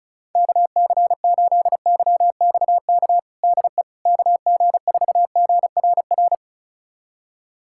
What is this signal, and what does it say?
Morse code, heard as a keyed tone, about 36 wpm, 700 Hz. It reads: KC8YXK DE KG4GRR
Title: USA Callsign DE Callsign - 6 Characters - 35wpm